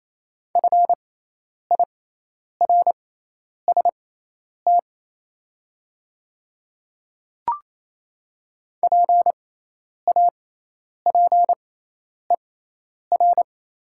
Morse code, heard as a keyed tone, about 28 wpm, spaced as for 10 wpm, 690 Hz. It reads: FIRST PAPER
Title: 6x Character Spacing - Top 300 Words - 28wpm